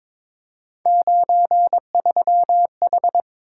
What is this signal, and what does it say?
Morse code, heard as a keyed tone, about 22 wpm, 700 Hz. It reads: 93H
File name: practice / 22wpm / Three Letter-Number Combinations - 22wpm